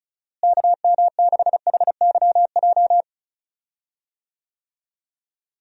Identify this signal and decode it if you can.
Morse code, heard as a keyed tone, about 34 wpm, 700 Hz. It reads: KM6HYJ
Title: USA Call Signs - 6 Characters - Repeated 3x - 35wpm